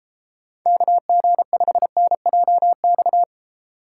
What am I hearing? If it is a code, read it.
Morse code, heard as a keyed tone, about 34 wpm, 700 Hz. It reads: KG5NJX